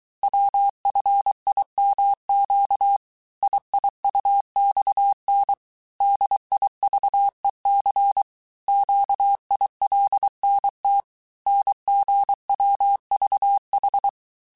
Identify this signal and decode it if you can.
Morse code, heard as a keyed tone, about 23 wpm, 800 Hz. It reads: WFIMQ IIUXN DIVEC QILNT NGWVH
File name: cours-du-mercredi / audios / cours-855-texte-chiffre-vitesse-22-mpm